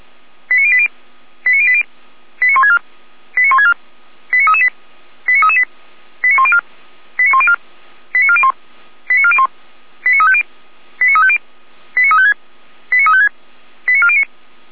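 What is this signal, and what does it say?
Five-tone selective call (ZVEI), heard as quick runs of five tones, tones of about 70 ms. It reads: 89080 89080 89164 89164 89338 89338 89104 89104 89401 89401 89350 89350 89356 89356 89409
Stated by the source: Received by SDR; ZVEI five-tone code 89235, Feuerwehr Orlamünde fire brigade.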